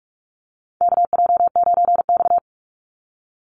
Morse code, heard as a keyed tone, about 46 wpm, 700 Hz. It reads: KJ9X
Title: NA SOTA Chaser Callsigns 45wpm